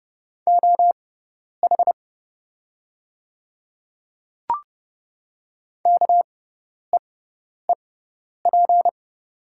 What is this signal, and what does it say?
Morse code, heard as a keyed tone, about 30 wpm, spaced as for 10 wpm, 700 Hz. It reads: OH KEEP
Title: 6x Character Spacing - Top 500 Words - 30wpm